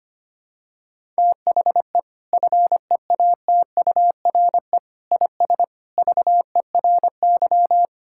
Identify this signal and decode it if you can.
Morse code, heard as a keyed tone, about 25 wpm, 700 Hz. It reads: THE FEATURE IS VERY
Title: Sentences from Top 1000 Words - Review Part 5 - 25wpm